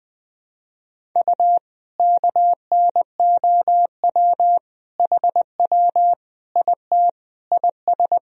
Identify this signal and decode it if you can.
Morse code, heard as a keyed tone, about 20 wpm, 700 Hz. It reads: U KNOW HW IT IS
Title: Sentences from Top 100 Words and Abbreviations 20wpm